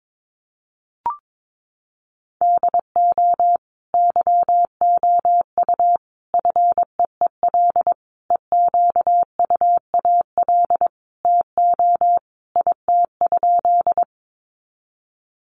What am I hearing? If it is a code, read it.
Morse code, heard as a keyed tone, about 22 wpm, 700 Hz. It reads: DO YOU FEEL EQUAL TO IT?
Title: Sentences from Top 1400 Words 22wpm